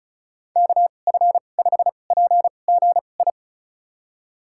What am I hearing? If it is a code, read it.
Morse code, heard as a keyed tone, about 34 wpm, spaced as for 25 wpm, 700 Hz. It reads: KF5PGI
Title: USA Call Signs - 6 Characters - 2x Character Spacing - 35wpm